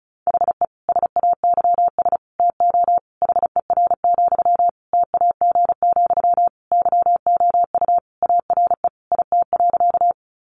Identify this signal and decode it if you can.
Morse code, heard as a keyed tone, about 34 wpm, 700 Hz. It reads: HE SAYS TO HER, TAG, YOU ARE IT.
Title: Additional Easy Phrases 35wpm